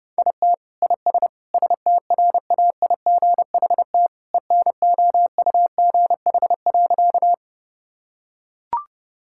Morse code, heard as a keyed tone, about 30 wpm, 700 Hz. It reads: IT IS STRAIGHT ENOUGH.
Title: Sentences from Top 1000 Words - Review Part 6 - 30wpm